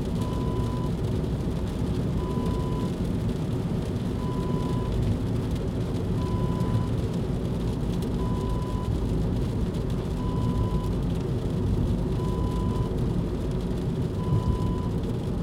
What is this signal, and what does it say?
There is no code.